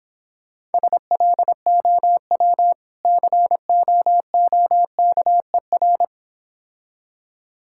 Morse code, heard as a keyed tone, about 26 wpm, 700 Hz. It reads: SLOW COOKER